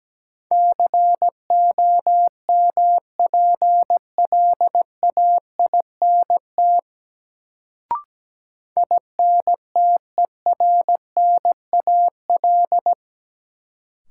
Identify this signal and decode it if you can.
Morse code, heard as a keyed tone, about 17 wpm, 700 Hz. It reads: COMPLAINT INTERNAL
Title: Sentences from Top 1400 Words - New Word Drill - 17wpm